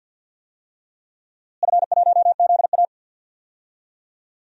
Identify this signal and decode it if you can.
Morse code, heard as a keyed tone, about 52 wpm, 700 Hz. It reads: F1ZA
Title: DXCC-100 Call Signs - 4 Characters - Repeated 3x - 50wpm